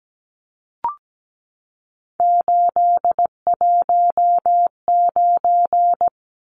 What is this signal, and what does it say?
Morse code, heard as a keyed tone, about 17 wpm, 700 Hz. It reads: 819
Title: Numbers - 3-Digit 17wpm